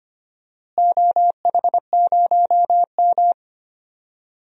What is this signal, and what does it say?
Morse code, heard as a keyed tone, about 25 wpm, 700 Hz. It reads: OH0M
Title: DXCC-100 Call Signs - 4 Characters - Repeated 3x - 25wpm